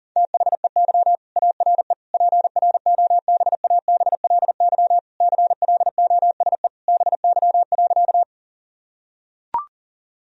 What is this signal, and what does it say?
Morse code, heard as a keyed tone, about 37 wpm, 700 Hz. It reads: THEY ARE PROBABLY CLOSE BY.
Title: Sentences from Top 1000 Words - Review Part 4 - 40wpm